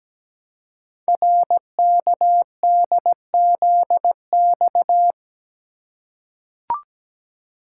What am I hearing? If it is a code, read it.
Morse code, heard as a keyed tone, about 17 wpm, 700 Hz. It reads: RKDZX
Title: Code Groups - 5 Letters - 17wpm